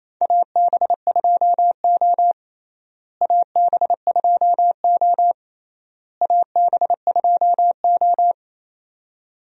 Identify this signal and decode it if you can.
Morse code, heard as a keyed tone, about 28 wpm, 700 Hz. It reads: AB2O AB2O AB2O
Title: USA Call Signs - 4 Characters - Repeated 3x - 28wpm